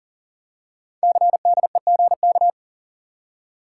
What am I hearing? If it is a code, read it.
Morse code, heard as a keyed tone, about 42 wpm, 700 Hz. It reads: CDEGK
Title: Code Groups - 5 Letters - 40wpm